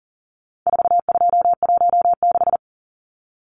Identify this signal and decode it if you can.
Morse code, heard as a keyed tone, about 40 wpm, 700 Hz. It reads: 4216
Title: Numbers - 4-Digit 40wpm